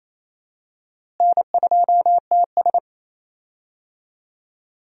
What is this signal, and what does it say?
Morse code, heard as a keyed tone, about 28 wpm, 700 Hz. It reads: N2TS
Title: USA Call Signs - 4 Characters - Repeated 3x - 28wpm